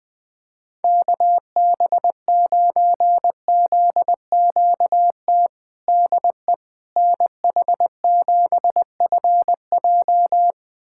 Morse code, heard as a keyed tone, about 20 wpm, 700 Hz. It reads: KB9ZQT DE NH7FJ
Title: USA Callsign DE Callsign 20wpm